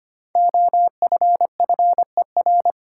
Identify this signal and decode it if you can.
Morse code, heard as a keyed tone, about 26 wpm, 700 Hz. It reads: OFFER
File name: 5 Letter Words - First 3 Letters Given - 25wpm